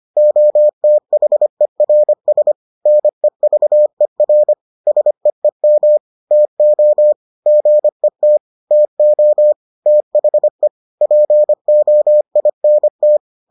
Morse code, heard as a keyed tone, about 25 wpm, 600 Hz. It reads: OTHERS NEVER SEEM TO GET TO THE POINT